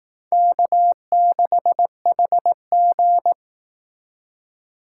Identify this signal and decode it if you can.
Morse code, heard as a keyed tone, about 18 wpm, 700 Hz. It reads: K6HG
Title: Speed Racing - USA 1x2 Call Signs 15wpm